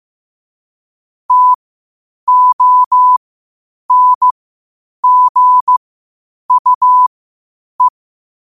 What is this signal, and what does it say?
Morse code, heard as a keyed tone, about 15 wpm, spaced as for 8 wpm, 1000 Hz. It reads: TONGUE
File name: qotdfiles / 8wpm